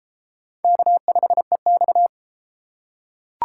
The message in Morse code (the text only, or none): K5EX